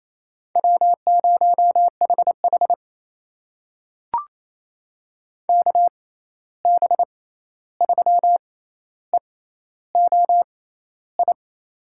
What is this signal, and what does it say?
Morse code, heard as a keyed tone, about 28 wpm, spaced as for 13 wpm, 700 Hz. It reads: W0HH KB3EOI
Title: USA Call Signs - 6x Character Spacing - 28wpm